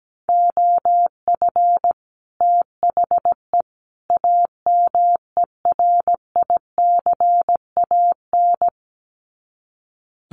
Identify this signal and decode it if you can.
Morse code, heard as a keyed tone, about 17 wpm, 700 Hz. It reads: OF THE AMERICAN
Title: Sets of 3 Words 17wpm